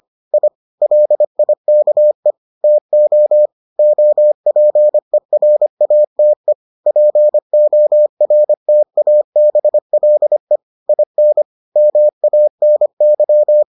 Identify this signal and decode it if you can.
Morse code, heard as a keyed tone, about 25 wpm, 600 Hz. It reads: I LIKE TO OPERATE PORTABLE IN MANY